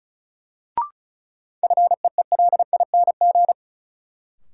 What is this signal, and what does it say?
Morse code, heard as a keyed tone, about 36 wpm, 700 Hz. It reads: FEELING
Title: Sentences from Top 1200 Words - New Word Drill Alt - 35wpm